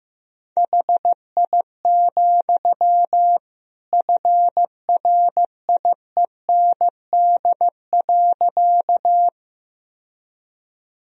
Morse code, heard as a keyed tone, about 16 wpm, 700 Hz. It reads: HI, FRIEND.
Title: Sentences from Top 1000 Words - Review Part 3 - 15wpm